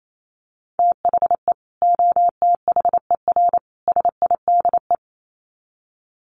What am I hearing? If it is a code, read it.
Morse code, heard as a keyed tone, about 28 wpm, 700 Hz. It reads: THE OTHER SIDE
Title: Sets of 3 Words 28wpm